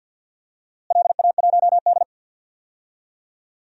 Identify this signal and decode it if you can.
Morse code, heard as a keyed tone, about 46 wpm, 700 Hz. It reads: LA1D
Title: DXCC-100 Call Signs - 4 Characters - Repeated 3x - 50wpm